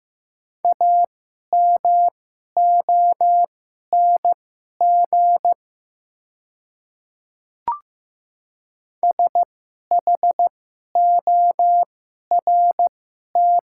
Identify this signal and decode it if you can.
Morse code, heard as a keyed tone, about 15 wpm, spaced as for 11 wpm, 690 Hz. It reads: AMONG SHORT
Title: 2x Character Spacing - Top 500 Words - 15wpm